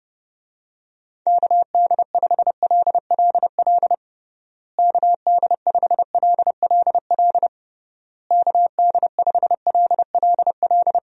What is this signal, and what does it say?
Morse code, heard as a keyed tone, about 32 wpm, 700 Hz. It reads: KD5LLL KD5LLL KD5LLL
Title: USA Call Signs - 6 Characters - Repeated 3x - 30wpm